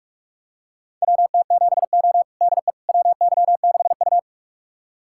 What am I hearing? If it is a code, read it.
Morse code, heard as a keyed tone, about 41 wpm, 700 Hz. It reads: WT7O DE WY6U
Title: USA Callsign DE Callsign - 4 Characters - 45wpm